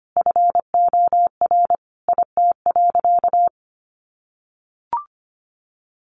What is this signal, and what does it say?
Morse code, heard as a keyed tone, about 25 wpm, 700 Hz. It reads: FOR IT.
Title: Sentences from Top 1100 Words 25wpm